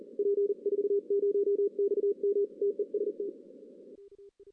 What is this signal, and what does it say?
Morse code, heard as a keyed tone, about 37 wpm, 400 Hz. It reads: P40XM TEST
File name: 2016 arrl dx cw / 2016-ARRL-DX-CW-K5TR- - 0-3